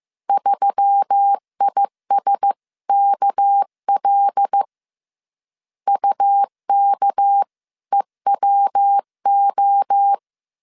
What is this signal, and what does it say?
Morse code, heard as a keyed tone, about 15 wpm, 800 Hz. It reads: 3ISKL UKEWO